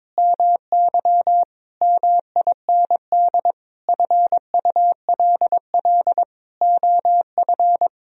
Morse code, heard as a keyed tone, about 22 wpm, 700 Hz. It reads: MY MIND FULL OF
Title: Sentences from Top 1000 Words - Review Part 3 - 22wpm